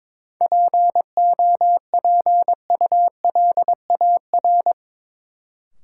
Morse code, heard as a keyed tone, about 22 wpm, 700 Hz. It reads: POPULAR
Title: Sentences from Top 1000 Words - Review Part 3 - New Word Drill - 22wpm